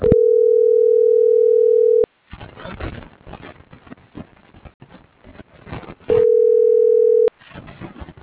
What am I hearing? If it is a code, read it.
Ringback tone (North America), 440 + 480 Hz.